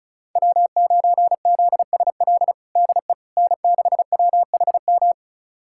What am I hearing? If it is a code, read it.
Morse code, heard as a keyed tone, about 36 wpm, 700 Hz. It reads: W9ZSL DE N6WHM
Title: USA Callsign DE Callsign - 5 Characters - 35wpm